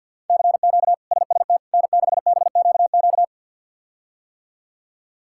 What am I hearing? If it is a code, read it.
Morse code, heard as a keyed tone, about 50 wpm, 700 Hz. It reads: CQ SST N6BQQ
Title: K1USN SST QSO Practice - 50wpm